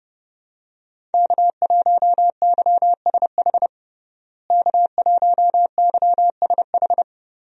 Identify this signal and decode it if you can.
Morse code, heard as a keyed tone, about 30 wpm, 700 Hz. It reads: K1YSH K1YSH
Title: USA Call Signs - 5 Characters - Repeated 2x - 30wpm